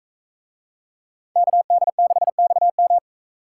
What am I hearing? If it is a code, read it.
Morse code, heard as a keyed tone, about 41 wpm, 700 Hz. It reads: KD6XM